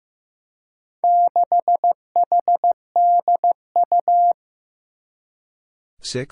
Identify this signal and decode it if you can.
Morse code, heard as a keyed tone, about 15 wpm, 700 Hz. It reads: 6HDU
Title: Four Challenging Letter-Number Combinations 15wpm